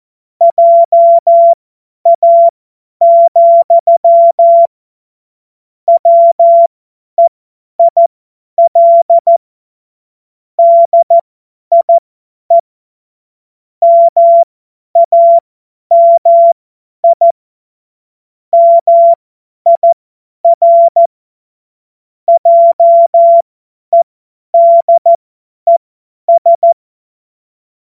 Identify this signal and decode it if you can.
Morse code, heard as a keyed tone, about 14 wpm, spaced as for 10 wpm, 680 Hz. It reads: JA, WEIL DIE MAMI MIR JEDES
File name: Morsekurs / Witze / Witz 31. Januar 2023 50 F70